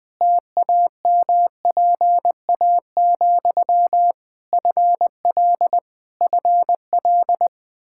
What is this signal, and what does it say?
Morse code, heard as a keyed tone, about 20 wpm, 700 Hz. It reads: TAMPA, FL FL
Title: Practice QSO - Exchange 1 - Beginning - 20wpm